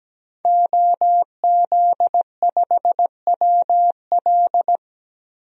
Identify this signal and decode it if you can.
Morse code, heard as a keyed tone, about 17 wpm, 700 Hz. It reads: OZ5WL
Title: DXCC-100 Call Signs - 5 Characters - 17wpm